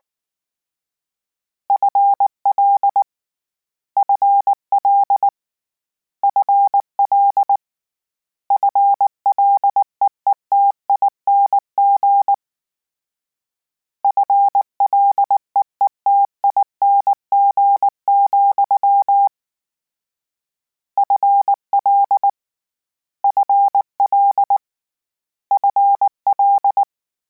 Morse code, heard as a keyed tone, about 19 wpm, 800 Hz. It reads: FL FL FL FLEETING FLEETING, FL FL FL